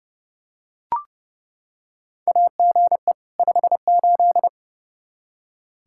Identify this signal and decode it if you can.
Morse code, heard as a keyed tone, about 32 wpm, 700 Hz. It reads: AGE 58
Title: Practice QSO - Exchange 3 - More Info - 30wpm